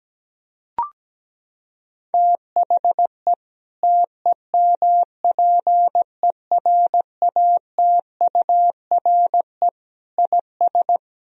Morse code, heard as a keyed tone, about 17 wpm, 700 Hz. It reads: THE TEMPERATURE IS
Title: Sentences from Top 1000 Words - Review Part 4 - 17wpm